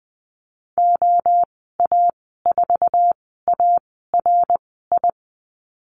Morse code, heard as a keyed tone, about 20 wpm, spaced as for 14 wpm, 700 Hz. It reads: OA4ARI
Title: 069 Lesson 12g - Letter d - Call Signs 20wpm